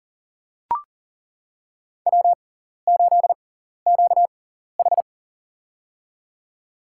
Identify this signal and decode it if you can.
Morse code, heard as a keyed tone, about 40 wpm, spaced as for 14 wpm, 700 Hz. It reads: W8QH
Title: USA Call Signs - 4 Characters - 6x Character Spacing - 40wpm